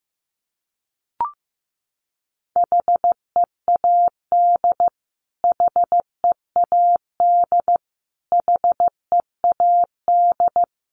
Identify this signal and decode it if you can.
Morse code, heard as a keyed tone, about 15 wpm, 700 Hz. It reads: HEAD HEAD HEAD
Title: Top 500 Words Repeated 15wpm